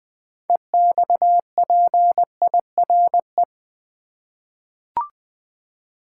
Morse code, heard as a keyed tone, about 20 wpm, 700 Hz. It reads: EXPIRE
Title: 6 Letter Words - First 3 Letters Given - 20wpm